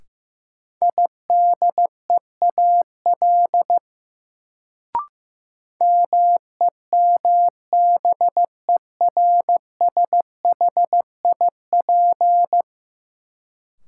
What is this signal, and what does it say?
Morse code, heard as a keyed tone, about 15 wpm, 700 Hz. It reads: IDEAL MEMBERSHIP
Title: Sentences from Top 1400 Words - New Word Drill - 15wpm